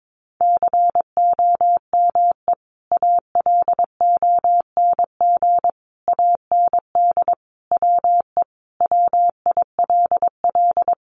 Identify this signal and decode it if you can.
Morse code, heard as a keyed tone, about 22 wpm, 700 Hz. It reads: COME ALONG AND WE WILL
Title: Sentences from Top 1500 Words 22wpm